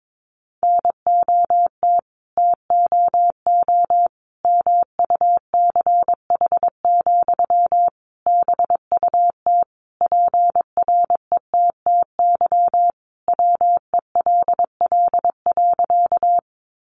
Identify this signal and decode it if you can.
Morse code, heard as a keyed tone, about 22 wpm, 700 Hz. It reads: NOT TOO MUCH, BUT PRETTY WELL.
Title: Sentences from Top 1000 Words 22wpm